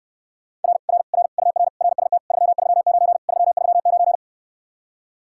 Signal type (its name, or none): Morse code